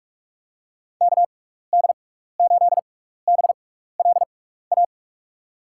Morse code, heard as a keyed tone, about 43 wpm, spaced as for 15 wpm, 700 Hz. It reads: KD8BLA
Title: USA Call Signs - 6 Characters - 6x Character Spacing - 45wpm